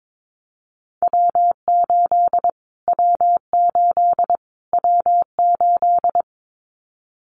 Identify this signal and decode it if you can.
Morse code, heard as a keyed tone, about 22 wpm, 700 Hz. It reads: W8 W8 W8
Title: Callsign Prefixes for Top 25 DXCC - Repeated 3x - 22wpm